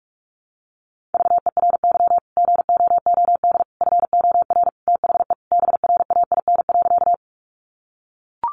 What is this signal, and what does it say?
Morse code, heard as a keyed tone, about 44 wpm, 700 Hz. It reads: VERY GOOD FOR THE BRAIN.
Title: Sentences from Top 1100 Words 45wpm